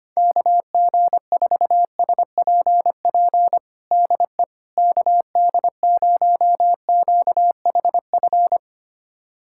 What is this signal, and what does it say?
Morse code, heard as a keyed tone, about 25 wpm, 700 Hz. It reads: KG4SPP DE KD0QHF
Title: USA Callsign DE Callsign - 6 Characters - 25wpm